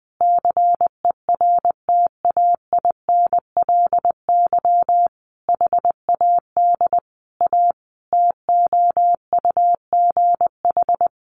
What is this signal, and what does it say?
Morse code, heard as a keyed tone, about 20 wpm, 700 Hz. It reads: CERTAINLY HAD A TOUGH
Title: Sentences from Top 1200 Words 20wpm